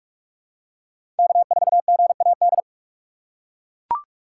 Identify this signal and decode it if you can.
Morse code, heard as a keyed tone, about 44 wpm, 690 Hz. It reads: K4GAD
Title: USA Call Signs - 5 Characters - 4x Character Spacing - 45wpm